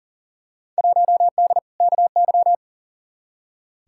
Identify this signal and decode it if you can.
Morse code, heard as a keyed tone, about 40 wpm, 700 Hz. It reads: 1D KY
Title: Field Day Exchanges - 40wpm